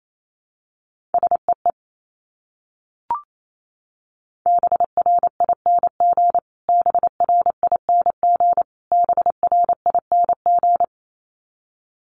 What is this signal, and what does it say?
Morse code, heard as a keyed tone, about 28 wpm, 700 Hz. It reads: SEE BRING BRING BRING
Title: Top 500 Words Repeated 28wpm